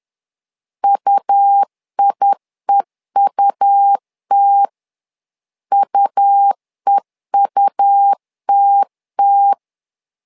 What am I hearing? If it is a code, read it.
Morse code, heard as a keyed tone, about 10 wpm, 800 Hz. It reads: UIEUT UEUTT
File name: LEZIONI / 2 UHDN / 2 U 10wpm 1622